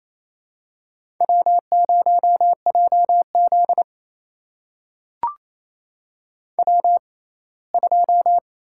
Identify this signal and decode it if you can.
Morse code, heard as a keyed tone, about 28 wpm, spaced as for 19 wpm, 700 Hz. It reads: W0JZ W2
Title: USA Call Signs - 4 Characters - 6x Character Spacing - 28wpm